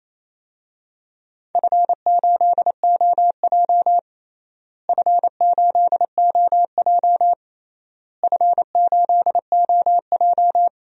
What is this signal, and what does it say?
Morse code, heard as a keyed tone, about 28 wpm, 700 Hz. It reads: F8OJ F8OJ F8OJ
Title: DXCC-100 Call Signs - 4 Characters - Repeated 3x - 28wpm